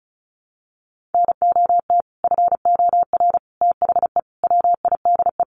Morse code, heard as a keyed tone, about 34 wpm, 700 Hz. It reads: NOT FOR THE WIDE
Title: Sentences from Top 900 Words 35wpm